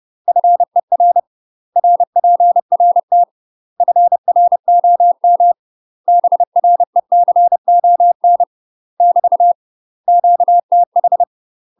Morse code, heard as a keyed tone, about 30 wpm, 700 Hz. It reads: FER RPRT FROM BRECON <BT> QTH
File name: UK MASTER QSOs TRAINING / 30WPM / UK MASTER QSOs 0051